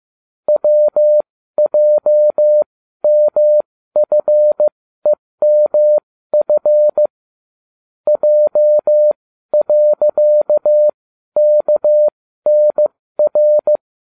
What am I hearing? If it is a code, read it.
Morse code, heard as a keyed tone, about 15 wpm, spaced as for 12 wpm, 600 Hz. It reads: WJMFEMF J.KNR